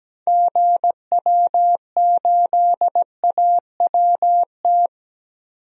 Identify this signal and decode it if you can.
Morse code, heard as a keyed tone, about 17 wpm, 700 Hz. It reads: GW8AWT
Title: DXCC-100 Call Signs - 6 Characters - 17wpm